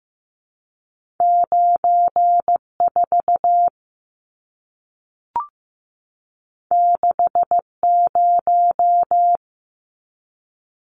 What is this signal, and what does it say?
Morse code, heard as a keyed tone, about 15 wpm, 700 Hz. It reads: 9 4 6 0
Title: Numbers - 2-Digit 15wpm